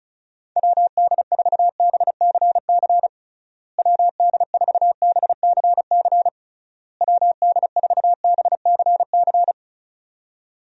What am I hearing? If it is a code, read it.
Morse code, heard as a keyed tone, about 36 wpm, 700 Hz. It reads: WD4BCC WD4BCC WD4BCC
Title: USA Call Signs - 6 Characters - Repeated 3x - 35wpm